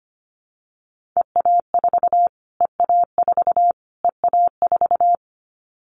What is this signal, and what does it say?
Morse code, heard as a keyed tone, about 25 wpm, 700 Hz. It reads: EA4 EA4 EA4
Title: Callsign Prefixes for Top 25 DXCC - Repeated 3x - 25wpm